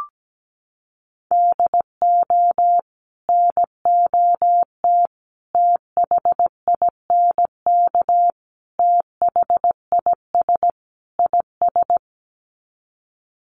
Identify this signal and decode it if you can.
Morse code, heard as a keyed tone, about 17 wpm, 700 Hz. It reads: DO NOT THINK THIS IS
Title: Sets of 5 Words Encore2 17wpm